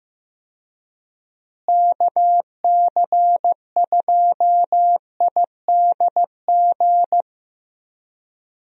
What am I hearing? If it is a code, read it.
Morse code, heard as a keyed tone, about 15 wpm, 700 Hz. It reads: KC2IDG
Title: USA Call Signs - 6 Characters - Repeated 2x - 15wpm